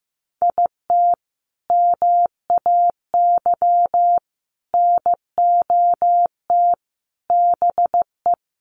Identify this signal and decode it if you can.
Morse code, heard as a keyed tone, about 15 wpm, 700 Hz. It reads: IT MAY NOT BE
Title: Sentences from Top 1400 Words 15wpm